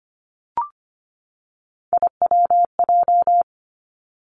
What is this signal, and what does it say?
Morse code, heard as a keyed tone, about 25 wpm, 700 Hz. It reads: IWJ
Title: Three Challenging Letter-Number Combinations 25wpm